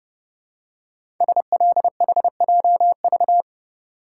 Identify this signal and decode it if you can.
Morse code, heard as a keyed tone, about 30 wpm, 700 Hz. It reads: SLHJV
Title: Code Groups - 5 Letters - 30wpm